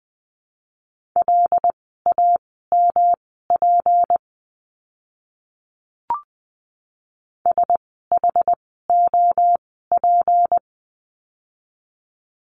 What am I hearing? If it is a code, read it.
Morse code, heard as a keyed tone, about 20 wpm, spaced as for 15 wpm, 700 Hz. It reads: LAMP SHOP
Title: 153 Lesson 25f - Letter p - Words 20wpm